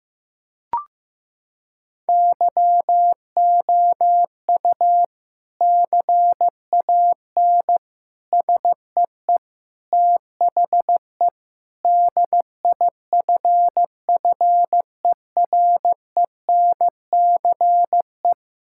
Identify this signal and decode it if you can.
Morse code, heard as a keyed tone, about 14 wpm, 700 Hz. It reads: YOU CAN SEE THE DIFFERENCE
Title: Sentences from Top 1000 Words - Review Part 5 - 15wpm